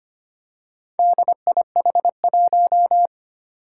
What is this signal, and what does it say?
Morse code, heard as a keyed tone, about 25 wpm, 700 Hz. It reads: DIH1
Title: Four Challenging Letter-Number Combinations 25wpm